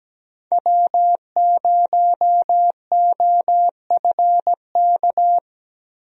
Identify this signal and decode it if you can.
Morse code, heard as a keyed tone, about 17 wpm, 700 Hz. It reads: W0OFK
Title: USA Call Signs - 5 Characters - 17wpm